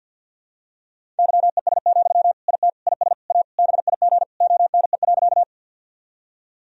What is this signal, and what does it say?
Morse code, heard as a keyed tone, about 49 wpm, 700 Hz. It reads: YES, IT IS A BIG ONE.
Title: Sentences from Top 1000 Words - Review Part 1 - 50wpm